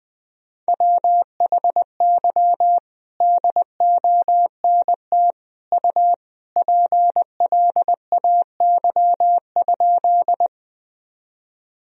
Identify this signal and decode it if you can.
Morse code, heard as a keyed tone, about 20 wpm, 700 Hz. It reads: WHY DONT U PLAY?
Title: Sentences from Top 300 Words and Abbreviations 20wpm